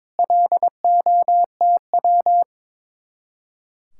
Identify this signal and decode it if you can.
Morse code, heard as a keyed tone, about 23 wpm, 700 Hz. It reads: LOTW
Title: Speed-Racing Common QSO Phrases Encore - 22wpm